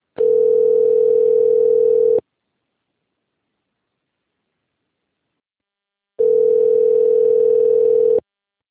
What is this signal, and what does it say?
Ringback tone (North America), 440 + 480 Hz.
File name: rg-1600-unknown- - - .16908